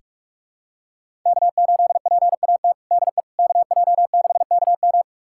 Morse code, heard as a keyed tone, about 43 wpm, 700 Hz. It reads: K8PAT DE KJ6KM
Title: USA Callsign DE Callsign - 5 Characters - 45wpm